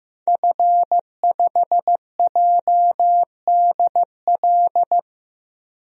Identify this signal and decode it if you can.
Morse code, heard as a keyed tone, about 15 wpm, 700 Hz. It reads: F5JDL
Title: DXCC-100 Call Signs - 5 Characters - Repeated 2x - 15wpm